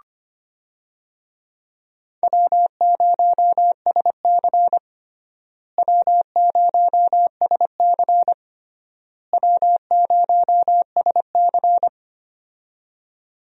Morse code, heard as a keyed tone, about 25 wpm, 700 Hz. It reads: W0SC W0SC W0SC
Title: USA Call Signs - 4 Characters - Repeated 3x - 25wpm